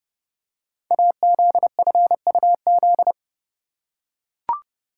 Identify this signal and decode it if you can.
Morse code, heard as a keyed tone, about 30 wpm, 700 Hz. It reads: AZFUZ
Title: Code Groups - 5 Letters - 30wpm